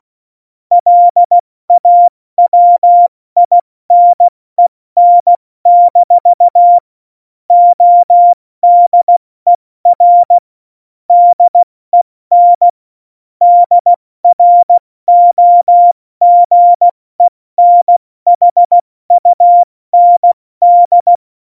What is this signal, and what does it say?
Morse code, heard as a keyed tone, about 16 wpm, 700 Hz. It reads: LAWINEN- ODER DEN DROGENHUND
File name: Morsekurs / Witze / Witz 28. Januar 2020, 70 F80